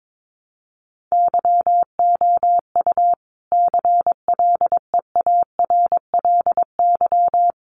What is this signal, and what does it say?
Morse code, heard as a keyed tone, about 22 wpm, 700 Hz. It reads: YOU CLEARLY
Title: Sentences from Top 1300 Words 22wpm